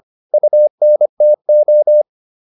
Morse code, heard as a keyed tone, about 25 wpm, 600 Hz. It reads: UNTO
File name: UN4 25